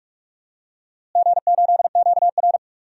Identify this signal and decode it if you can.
Morse code, heard as a keyed tone, about 46 wpm, 700 Hz. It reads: G8QR
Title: DXCC-100 Call Signs - 4 Characters - Repeated 3x - 45wpm